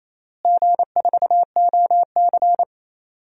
Morse code, heard as a keyed tone, about 27 wpm, 700 Hz. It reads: G4OC
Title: DXCC-100 Call Signs - 4 Characters - Repeated 3x - 28wpm